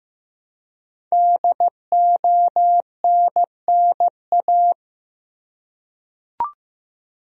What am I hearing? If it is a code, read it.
Morse code, heard as a keyed tone, about 15 wpm, 700 Hz. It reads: DONNA
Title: USA Common Names - 15wpm